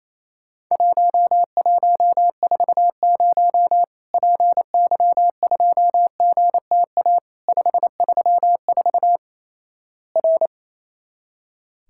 Morse code, heard as a keyed tone, about 28 wpm, 700 Hz. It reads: 1140 PY2GTA 534 R
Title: Worked All Europe DX Contest - QTC Exchange - 28wpm